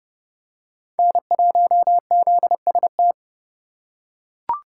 Morse code, heard as a keyed tone, about 27 wpm, 700 Hz. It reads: N1ZST